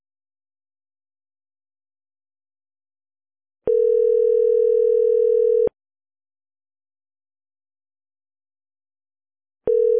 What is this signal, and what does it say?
Ringback tone (North America), 440 + 480 Hz.